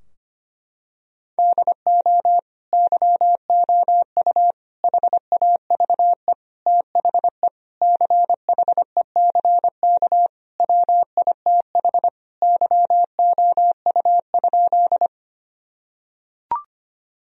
Morse code, heard as a keyed tone, about 25 wpm, 700 Hz. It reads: DO YOU HAVE THE CHECK WITH YOU?
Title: Sentences from Top 1000 Words - Review Part 3 - 25wpm